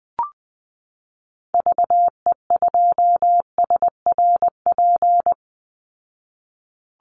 Morse code, heard as a keyed tone, about 20 wpm, 700 Hz. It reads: VE2SRP
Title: Call Signs Repeated 3x - 20wpm